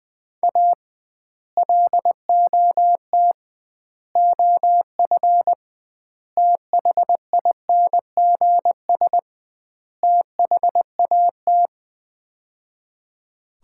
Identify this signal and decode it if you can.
Morse code, heard as a keyed tone, about 20 wpm, 700 Hz. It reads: A LOT OF THINGS THAT
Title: 2x Word-Spacing - Sets of 5 Words - 20wpm